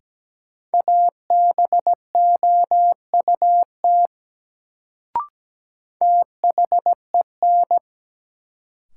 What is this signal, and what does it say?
Morse code, heard as a keyed tone, about 17 wpm, 700 Hz. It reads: ABOUT THEN
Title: Sentences from Top 100 Words - New Word Drill - 17wpm